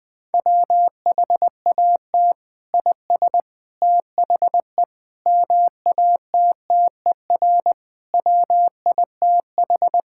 Morse code, heard as a keyed tone, about 20 wpm, 700 Hz. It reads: WHAT IS THE MATTER WITH
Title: Sentences from Top 1000 Words - Review Part 5 - 20wpm